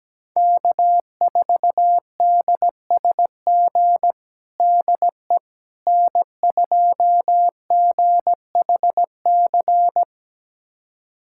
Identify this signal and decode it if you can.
Morse code, heard as a keyed tone, about 17 wpm, 700 Hz. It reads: K4DSG DE N2GHC
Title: USA Callsign DE Callsign - 5 Characters - 17wpm